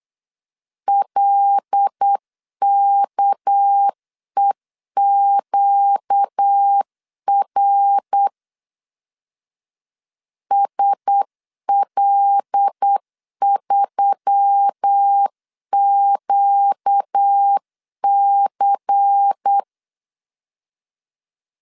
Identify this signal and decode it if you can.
Morse code, heard as a keyed tone, about 8 wpm, 800 Hz. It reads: LKEQR SL3QC
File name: CW / DET / 1 6 1500C S C 8wpm 5219